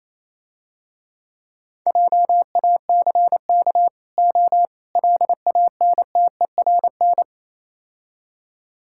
Morse code, heard as a keyed tone, about 28 wpm, 700 Hz. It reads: JACK O LANTERN